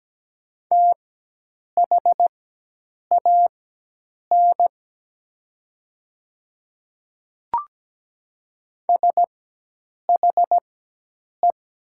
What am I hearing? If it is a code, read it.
Morse code, heard as a keyed tone, about 17 wpm, spaced as for 8 wpm, 700 Hz. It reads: THAN SHE
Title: 4x Character Spacing - Top 100 Words - 17wpm